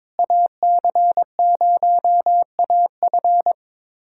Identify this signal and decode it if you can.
Morse code, heard as a keyed tone, about 23 wpm, 700 Hz. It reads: AC0AF
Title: USA Callsign DE Callsign - 5 Characters - 22wpm